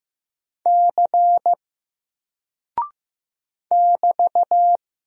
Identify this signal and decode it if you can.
Morse code, heard as a keyed tone, about 15 wpm, 700 Hz. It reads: C <BT>
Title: Single Character (Letter:Number:Punctuation:Prosign) Alt 15wpm